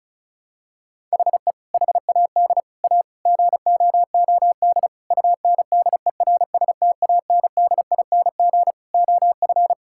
Morse code, heard as a keyed tone, about 36 wpm, 700 Hz. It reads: HE HAD A GOOD UNDERSTANDING OF
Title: Sentences from Top 1000 Words - Review Part 6 - 35wpm